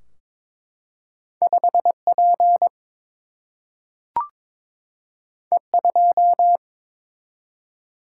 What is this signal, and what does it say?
Morse code, heard as a keyed tone, about 22 wpm, 700 Hz. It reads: 5 P E 2
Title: Single-Letter Single-Number 22wpm